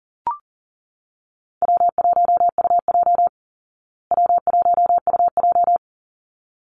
Morse code, heard as a keyed tone, about 40 wpm, 700 Hz. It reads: W1UJ W1UJ
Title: Call Signs Repeated 3x - 40wpm